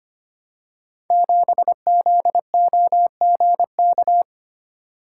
Morse code, heard as a keyed tone, about 25 wpm, 700 Hz. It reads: 7ZOGK
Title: Code Groups - 5 Letter:Numbers - 25wpm